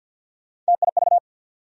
Morse code, heard as a keyed tone, about 47 wpm, 700 Hz. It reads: TIV
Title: USA Call Sign Postfixes - 3 Letters 50wpm